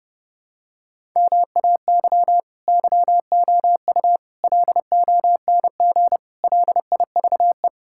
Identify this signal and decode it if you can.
Morse code, heard as a keyed tone, about 30 wpm, 700 Hz. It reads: MAY YOU LONG LIVE
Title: Sentences from Top 1000 Words - Review Part 6 - 30wpm